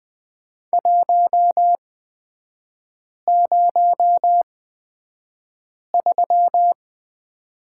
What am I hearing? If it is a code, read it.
Morse code, heard as a keyed tone, about 20 wpm, spaced as for 5 wpm, 700 Hz. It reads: 103
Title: 236 Lesson 38e - Letter q - Three Characters 20wpm